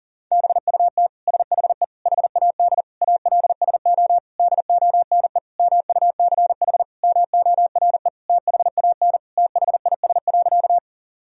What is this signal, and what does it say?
Morse code, heard as a keyed tone, about 40 wpm, 700 Hz. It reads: BUT SHE HAD ALSO DONE MUCH MORE THAN THIS.
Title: Sentences from Top 1000 Words - Review Part 1 - 40wpm